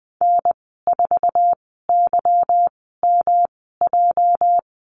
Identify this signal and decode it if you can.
Morse code, heard as a keyed tone, about 20 wpm, spaced as for 14 wpm, 700 Hz. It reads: N4YMJ
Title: 238 Lesson 38g - Letter q - Call Signs 20wpm